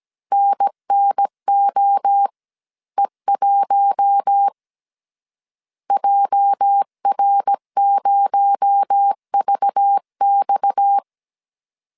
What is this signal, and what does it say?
Morse code, heard as a keyed tone, about 17 wpm, 800 Hz. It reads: NNOE1 JR0VX
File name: CW / DET / 1 9 1500C S C 16wpm 2909